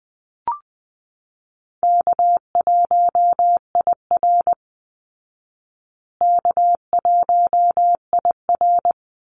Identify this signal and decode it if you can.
Morse code, heard as a keyed tone, about 20 wpm, 700 Hz. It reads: K1IR K1IR
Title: Call Signs Repeated 3x - 20wpm